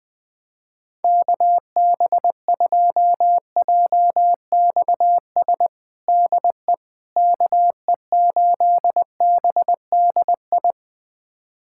Morse code, heard as a keyed tone, about 21 wpm, 700 Hz. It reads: KB2JXS DE KE8BDI
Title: USA Callsign DE Callsign 20wpm